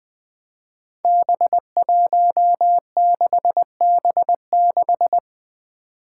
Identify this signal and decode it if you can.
Morse code, heard as a keyed tone, about 20 wpm, 700 Hz. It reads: B16B6
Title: Five Challenging Letter-Number Combinations 20wpm